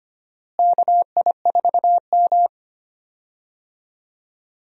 Morse code, heard as a keyed tone, about 25 wpm, 700 Hz. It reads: KI4M
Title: USA Call Signs - 4 Characters - Repeated 2x - 25wpm